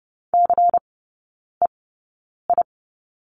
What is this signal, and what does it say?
Morse code, heard as a keyed tone, about 28 wpm, spaced as for 9 wpm, 690 Hz. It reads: CEI
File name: Lesson 16e - Letter c - Three Characters 30wpm